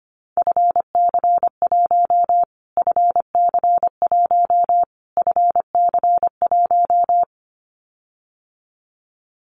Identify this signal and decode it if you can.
Morse code, heard as a keyed tone, about 25 wpm, 700 Hz. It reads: FC1 FC1 FC1
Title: Callsign Prefixes for Top 25 DXCC - Repeated 3x - 25wpm